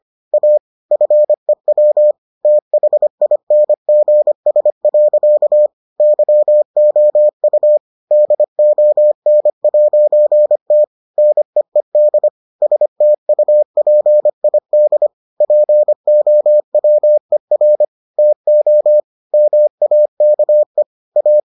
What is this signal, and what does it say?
Morse code, heard as a keyed tone, about 25 wpm, 600 Hz. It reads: A FEW THINGS. YOU DON'T NEED STUPID POWER TO MAKE A